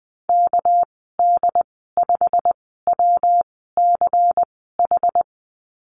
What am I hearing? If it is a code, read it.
Morse code, heard as a keyed tone, about 20 wpm, spaced as for 15 wpm, 700 Hz. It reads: KD5WCH
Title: 213 Lesson 34g - Number 8 - Call Signs 20wpm